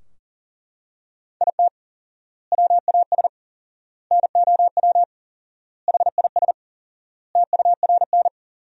Morse code, heard as a keyed tone, about 40 wpm, 700 Hz. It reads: IT WAS NOW HIS TURN
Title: 4x Word-Spacing - Sentences from Top 200 Words - 40wpm